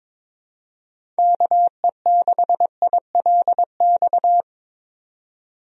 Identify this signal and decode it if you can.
Morse code, heard as a keyed tone, about 22 wpm, 700 Hz. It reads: KE6ILX
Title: USA Call Signs - 6 Characters - 22wpm